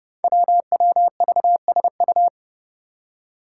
Morse code, heard as a keyed tone, about 30 wpm, 700 Hz. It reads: WWVSU
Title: Five Challenging Letter-Number Combinations 30wpm